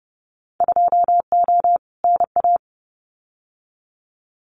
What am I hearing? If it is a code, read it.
Morse code, heard as a keyed tone, about 31 wpm, 700 Hz. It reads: 2O NA
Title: Winter Field Day Exchanges 30wpm